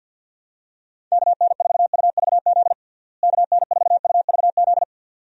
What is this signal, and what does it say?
Morse code, heard as a keyed tone, about 48 wpm, 700 Hz. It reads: KN4UVZ KN4UVZ
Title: USA Call Signs - 6 Characters - Repeated 2x - 50wpm